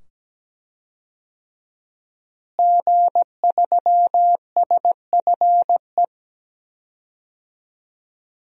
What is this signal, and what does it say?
Morse code, heard as a keyed tone, about 17 wpm, 700 Hz. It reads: G3SFE
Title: DXCC-100 Call Signs - 5 Characters - Repeated 2x - 17wpm